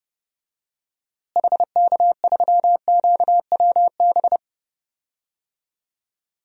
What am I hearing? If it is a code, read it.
Morse code, heard as a keyed tone, about 32 wpm, 700 Hz. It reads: HK3QWB
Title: DXCC-100 Call Signs - 6 Characters - Repeated 2x - 30wpm